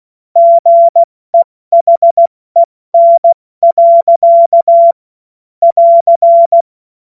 Morse code, heard as a keyed tone, about 16 wpm, spaced as for 14 wpm, 680 Hz. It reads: GEHEN. <AR>
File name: Witz-08-Juli-2025-70-F80